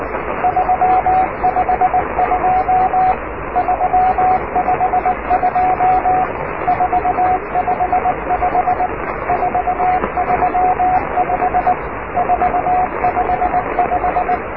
Morse code, heard as a keyed tone, about 19 wpm, 750 Hz, with radio noise. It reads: 352 352 455 435 455
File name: oscar 7 de k5dze section 2